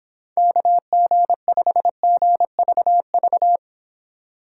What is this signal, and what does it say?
Morse code, heard as a keyed tone, about 30 wpm, 700 Hz. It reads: KG5GVV